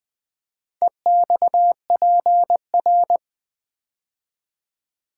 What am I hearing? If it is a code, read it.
Morse code, heard as a keyed tone, about 20 wpm, 700 Hz. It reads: EXPR
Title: Build A Word - 4 Letters At A Time - 10-Letter Words - 20wpm